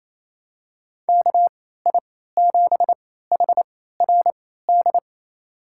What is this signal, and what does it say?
Morse code, heard as a keyed tone, about 28 wpm, spaced as for 16 wpm, 700 Hz. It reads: KI7HRD
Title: USA Call Signs - 6 Characters - 3x Character Spacing - 28wpm